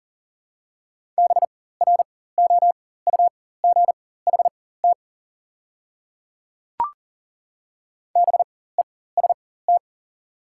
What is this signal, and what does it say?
Morse code, heard as a keyed tone, about 40 wpm, spaced as for 19 wpm, 690 Hz. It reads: BROUGHT BEST
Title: 4x Character Spacing - Top 500 Words - 40wpm